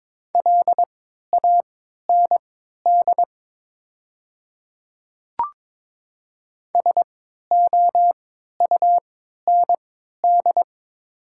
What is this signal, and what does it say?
Morse code, heard as a keyed tone, about 22 wpm, spaced as for 12 wpm, 700 Hz. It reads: LAND SOUND
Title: 3x Character Spacing - Top 200 Words - 22wpm